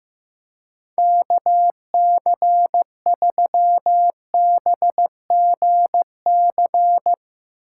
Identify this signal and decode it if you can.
Morse code, heard as a keyed tone, about 16 wpm, 700 Hz. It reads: KC3BGC
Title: Speed Racing - USA 2x3 Call Signs 15wpm